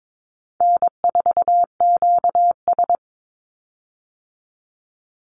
Morse code, heard as a keyed tone, about 22 wpm, 700 Hz. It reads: N4QS
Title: Easy Call Signs - 22wpm